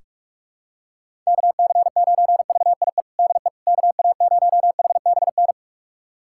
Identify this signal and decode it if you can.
Morse code, heard as a keyed tone, about 46 wpm, 700 Hz. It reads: KC9VIE DE KA0HBN
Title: USA Callsign DE Callsign - 6 Characters - 45wpm